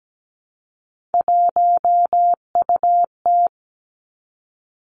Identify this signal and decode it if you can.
Morse code, heard as a keyed tone, about 17 wpm, 700 Hz. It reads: 1UT
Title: Three Challenging Letter-Number Combinations 17wpm